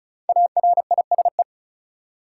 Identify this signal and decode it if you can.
Morse code, heard as a keyed tone, about 36 wpm, 700 Hz. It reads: ARISE
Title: Five Letter Words - 35wpm